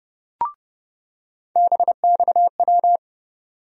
Morse code, heard as a keyed tone, about 30 wpm, 700 Hz. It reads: BXW
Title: USA Call Sign Postfixes - 3 Letters 30wpm